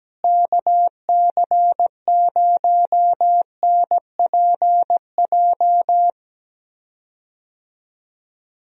Morse code, heard as a keyed tone, about 17 wpm, 700 Hz. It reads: KC0NPJ
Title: USA Call Signs - 6 Characters - Repeated 3x - 17wpm